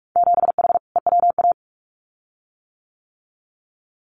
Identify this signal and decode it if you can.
Morse code, heard as a keyed tone, about 46 wpm, 700 Hz. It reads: 7H EWA
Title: Winter Field Day Exchanges Encore 45wpm